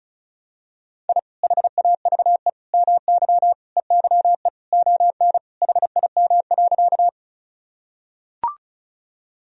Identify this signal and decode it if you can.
Morse code, heard as a keyed tone, about 36 wpm, 700 Hz. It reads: I HAVE MY EYE ON HIM.
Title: Sentences from Top 1000 Words - Review Part 1 - 35wpm